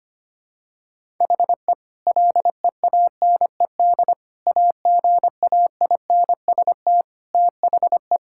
Morse code, heard as a keyed tone, about 25 wpm, 700 Hz. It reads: HE LEANED AGAINST THE
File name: practice / 25wpm / Sentences with 2-Word Phrases - 25wpm